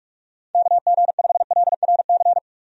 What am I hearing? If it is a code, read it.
Morse code, heard as a keyed tone, about 47 wpm, 700 Hz. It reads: KG5LRC
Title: USA Call Signs - 6 Characters - 45wpm